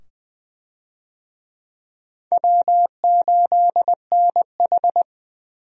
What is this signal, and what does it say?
Morse code, heard as a keyed tone, about 20 wpm, 700 Hz. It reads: W8NH
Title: USA Call Signs - 4 Characters - 20wpm